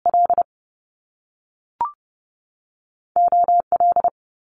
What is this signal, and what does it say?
Morse code, heard as a keyed tone, about 30 wpm, 700 Hz. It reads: L O L
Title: Top Two Letter Combinations in Words 30wpm